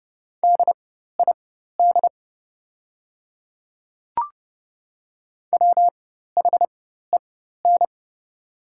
Morse code, heard as a keyed tone, about 30 wpm, spaced as for 14 wpm, 700 Hz. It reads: DID WHEN
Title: 4x Character Spacing - Top 500 Words - 30wpm